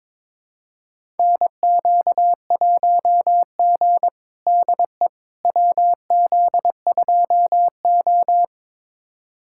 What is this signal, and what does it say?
Morse code, heard as a keyed tone, about 21 wpm, 700 Hz. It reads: NQ1G DE WZ2O
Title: USA Callsign DE Callsign - 4 Characters - 22wpm